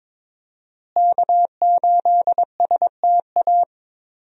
Morse code, heard as a keyed tone, about 22 wpm, 700 Hz. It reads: K8STA
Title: USA Call Signs - 5 Characters - Repeated 2x - 22wpm